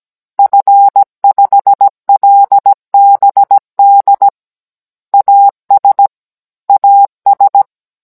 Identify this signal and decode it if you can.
Morse code, heard as a keyed tone, about 17 wpm, 800 Hz. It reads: F5LBD AS AS